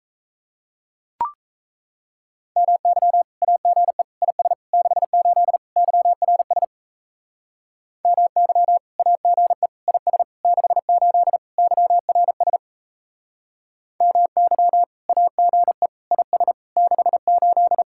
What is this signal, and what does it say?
Morse code, heard as a keyed tone, about 37 wpm, 700 Hz. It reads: MY AGE IS 68 YRS MY AGE IS 68 YRS MY AGE IS 68